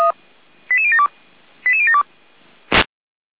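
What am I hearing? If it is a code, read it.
Five-tone selective call (ZVEI), heard as quick runs of five tones, tones of about 70 ms. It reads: 88072 88072
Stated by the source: ZVEI five-tone selective call received on an SDR.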